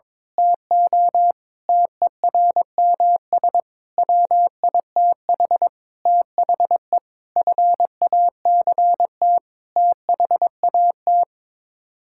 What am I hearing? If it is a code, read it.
Morse code, heard as a keyed tone, about 22 wpm, 700 Hz. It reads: TO TERMS WITH THE FACT THAT
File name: practice / 22wpm / Sets of 8 Words - 22wpm